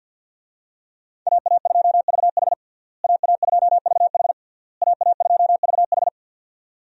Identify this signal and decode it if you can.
Morse code, heard as a keyed tone, about 48 wpm, 700 Hz. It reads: AA2VH AA2VH AA2VH
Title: USA Call Signs - 5 Characters - Repeated 3x - 50wpm